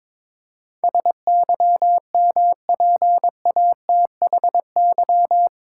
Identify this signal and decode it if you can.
Morse code, heard as a keyed tone, about 22 wpm, 700 Hz. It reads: SYMPATHY
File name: practice / 22wpm / Eight Letter Words - 22wpm